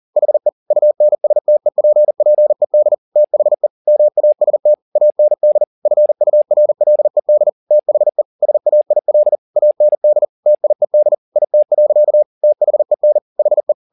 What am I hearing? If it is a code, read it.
Morse code, heard as a keyed tone, about 40 wpm, 600 Hz. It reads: HE UNSTEPPED THE MAST AND FURLED THE SAIL AND TIED IT. THEN HE
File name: the old man and the sea 2 40wpm